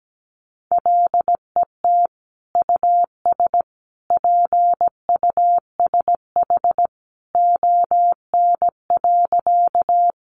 Morse code, heard as a keyed tone, about 17 wpm, 700 Hz. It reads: LET US PUSH ON.
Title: Sentences from Top 1300 Words 17wpm